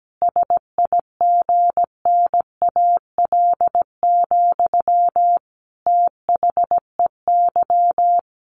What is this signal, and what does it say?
Morse code, heard as a keyed tone, about 17 wpm, 700 Hz. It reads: SIGNAL, THEY
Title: Sentences from Top 1400 Words 17wpm